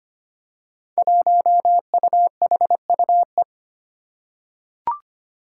Morse code, heard as a keyed tone, about 25 wpm, 700 Hz. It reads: 1UHUE
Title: Five Challenging Letter-Number Combinations 25wpm